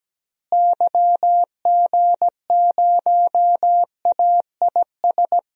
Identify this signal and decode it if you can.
Morse code, heard as a keyed tone, about 17 wpm, 700 Hz. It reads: YG0AIS
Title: DXCC-100 Call Signs - 6 Characters - Repeated 3x - 17wpm